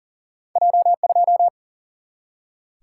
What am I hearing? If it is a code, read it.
Morse code, heard as a keyed tone, about 38 wpm, 700 Hz. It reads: J2
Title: Single-Letter Single-Number 40wpm